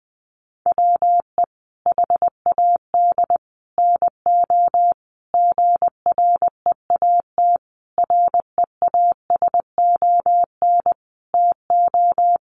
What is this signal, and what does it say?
Morse code, heard as a keyed tone, about 20 wpm, 700 Hz. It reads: WE HAD NO GREAT REASON TO
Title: Sentences from Top 1200 Words 20wpm